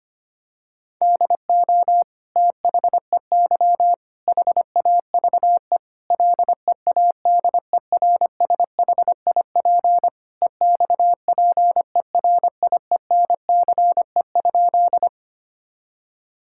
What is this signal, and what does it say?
Morse code, heard as a keyed tone, about 25 wpm, 700 Hz. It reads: DO THEY HAVE LEADERSHIP EXPERIENCE?
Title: Sentences from Top 1600 Words 25wpm